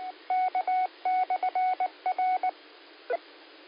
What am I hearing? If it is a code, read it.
Morse code, heard as a keyed tone, about 19 wpm, 740 Hz. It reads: K/R
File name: qsorec SimplexLogic 05-58-05